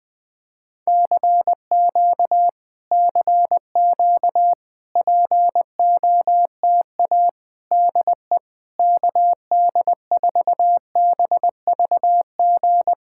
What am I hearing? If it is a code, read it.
Morse code, heard as a keyed tone, about 20 wpm, 700 Hz. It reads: CQ CQ POTA DE KD4BVG
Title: POTA QSO Practice - 20wpm